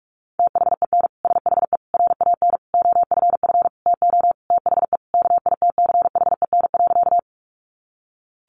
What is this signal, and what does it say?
Morse code, heard as a keyed tone, about 44 wpm, 700 Hz. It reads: THEN SHE RAN OFF TO THE KITCHEN.
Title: Sentences from Top 1400 Words 45wpm